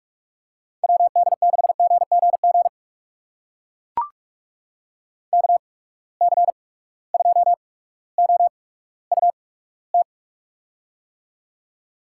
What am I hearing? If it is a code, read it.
Morse code, heard as a keyed tone, about 44 wpm, spaced as for 19 wpm, 700 Hz. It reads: WD6GGG KC2OUT
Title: USA Call Signs - 6 Characters - 8x Character Spacing - 45wpm